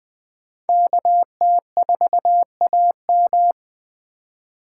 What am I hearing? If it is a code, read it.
Morse code, heard as a keyed tone, about 21 wpm, 700 Hz. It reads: KT4AM